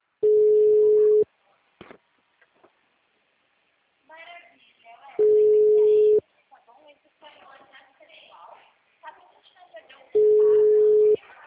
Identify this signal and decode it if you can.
Ringback tone (Europe, 425 Hz), with speech.